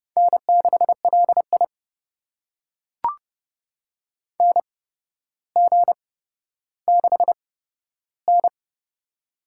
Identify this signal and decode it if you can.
Morse code, heard as a keyed tone, about 32 wpm, spaced as for 13 wpm, 700 Hz. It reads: N6LI NG6N
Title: USA Call Signs - 4 Characters - 8x Character Spacing - 30wpm